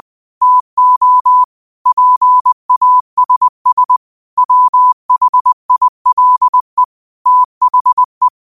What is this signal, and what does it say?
Morse code, heard as a keyed tone, about 20 wpm, 1000 Hz. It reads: TO PASS WHILE THE